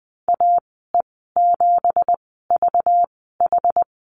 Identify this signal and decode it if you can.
Morse code, heard as a keyed tone, about 20 wpm, spaced as for 14 wpm, 700 Hz. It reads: AE7VH
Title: 193 Lesson 31g - Letter v - Call Signs 20wpm